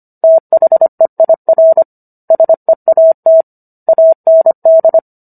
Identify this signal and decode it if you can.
Morse code, heard as a keyed tone, about 25 wpm, 650 Hz. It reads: THEIR SEAT AND
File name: Book One / 16-Chapter 16 - 25 WPM